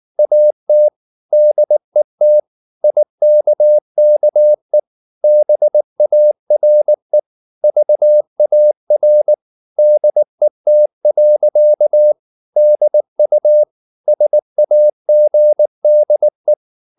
Morse code, heard as a keyed tone, about 19 wpm, 600 Hz. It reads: AT DET IKKE BARE VAR DET. DU SAGDE